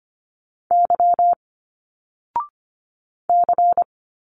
Morse code, heard as a keyed tone, about 25 wpm, 700 Hz. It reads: Y C
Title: Single Letters 25wpm